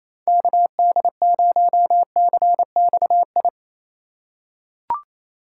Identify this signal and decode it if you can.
Morse code, heard as a keyed tone, about 27 wpm, 700 Hz. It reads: KD0CXI
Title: USA Call Signs - 6 Characters - 2x Character Spacing - 28wpm